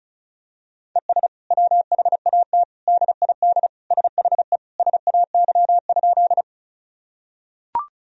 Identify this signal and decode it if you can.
Morse code, heard as a keyed tone, about 36 wpm, 700 Hz. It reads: ES WHAT DID SHE SAY?
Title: Sentences from Top 200 Words and Abbreviations 35wpm